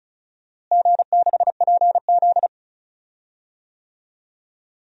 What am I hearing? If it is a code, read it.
Morse code, heard as a keyed tone, about 36 wpm, 690 Hz. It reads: G6PZ
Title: DXCC-100 Call Signs - 4 Characters - Repeated 3x - 35wpm